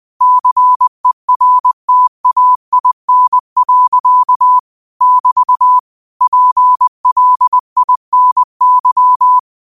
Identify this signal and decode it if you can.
Morse code, heard as a keyed tone, about 20 wpm, 1000 Hz. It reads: CERTAIN. <BT> PLINY
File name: qotdfiles / 20wpm